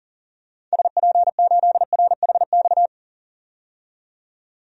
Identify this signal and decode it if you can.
Morse code, heard as a keyed tone, about 40 wpm, 700 Hz. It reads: SP8RHX
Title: DXCC-100 Call Signs - 6 Characters - 40wpm